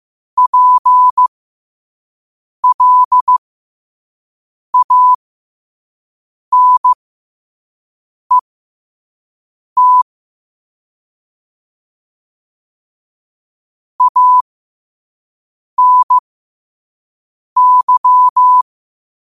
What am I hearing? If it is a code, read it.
Morse code, heard as a keyed tone, about 15 wpm, spaced as for 5 wpm, 1000 Hz. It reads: PLANET ANY
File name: qotdfiles / 5wpm